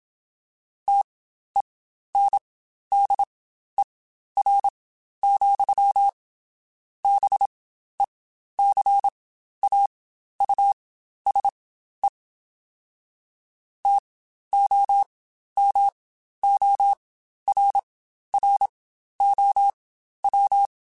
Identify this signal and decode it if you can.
Morse code, heard as a keyed tone, about 27 wpm, spaced as for 12 wpm, 800 Hz. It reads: TENDER, BECAUSE TOMORROW
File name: Spaced / Chuckles Spaced-mp3s / SoftWords sp @25